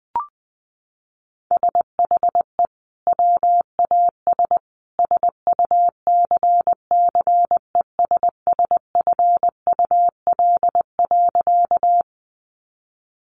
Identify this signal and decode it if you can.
Morse code, heard as a keyed tone, about 20 wpm, 700 Hz. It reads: SHE WAS SUCCESSFUL.
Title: Sentences from Top 900 Words 20wpm